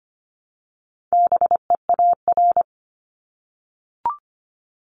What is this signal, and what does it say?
Morse code, heard as a keyed tone, about 25 wpm, 700 Hz. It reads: BEAR
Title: Multi-letter Phonemes 25wpm